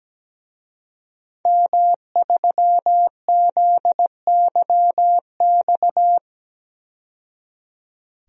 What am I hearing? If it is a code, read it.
Morse code, heard as a keyed tone, about 17 wpm, 700 Hz. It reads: M3ZYX
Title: DXCC-100 Call Signs - 5 Characters - 17wpm